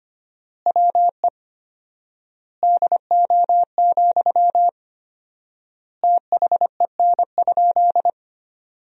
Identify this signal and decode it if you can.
Morse code, heard as a keyed tone, about 25 wpm, 700 Hz. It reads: WE DO, THEN?
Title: 4x Word-Spacing - Sentences from Top 100 Words - 25wpm